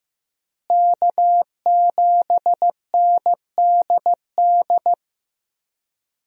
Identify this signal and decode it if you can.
Morse code, heard as a keyed tone, about 16 wpm, 700 Hz. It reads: K7NDD
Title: USA Call Signs - 5 Characters - Repeated 3x - 15wpm